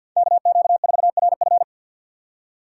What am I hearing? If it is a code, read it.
Morse code, heard as a keyed tone, about 53 wpm, 700 Hz. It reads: KQ4RF
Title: USA Call Signs - 5 Characters - 4x Character Spacing - 50wpm